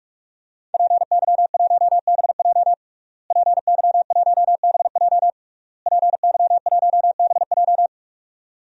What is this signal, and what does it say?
Morse code, heard as a keyed tone, about 41 wpm, 700 Hz. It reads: PY1BJ PY1BJ PY1BJ
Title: DXCC-100 Call Signs - 5 Characters - Repeated 3x - 45wpm